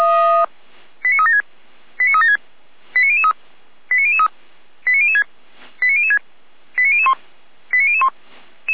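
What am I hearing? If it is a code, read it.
Five-tone selective call (ZVEI), heard as quick runs of five tones, tones of about 70 ms. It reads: 89376 89376 89003 89003 89006 89006 89001 89001
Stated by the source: ZVEI five-tone selective call 89150 received by SDR.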